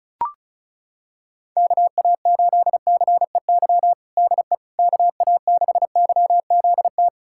Morse code, heard as a keyed tone, about 36 wpm, 700 Hz. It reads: KA8CEY DE KA6YZT
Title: USA Callsign DE Callsign - 6 Characters - 35wpm